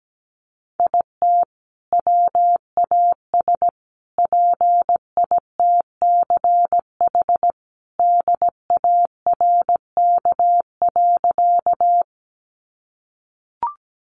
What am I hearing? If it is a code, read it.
Morse code, heard as a keyed tone, about 17 wpm, 700 Hz. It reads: IT WAS PITCH DARK.
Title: Sentences from Top 1300 Words 17wpm